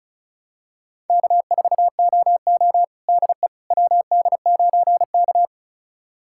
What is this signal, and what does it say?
Morse code, heard as a keyed tone, about 36 wpm, 700 Hz. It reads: K4OO DE WD9K
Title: USA Callsign DE Callsign - 4 Characters - 35wpm